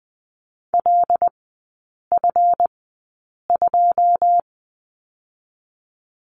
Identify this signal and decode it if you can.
Morse code, heard as a keyed tone, about 20 wpm, spaced as for 8 wpm, 700 Hz. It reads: LF2
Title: 159 Lesson 26e - Letter g - Three Characters 20wpm